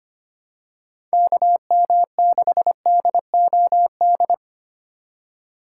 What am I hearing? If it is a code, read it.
Morse code, heard as a keyed tone, about 25 wpm, 700 Hz. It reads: KM6DOD
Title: USA Call Signs - 4x Character Spacing - 25wpm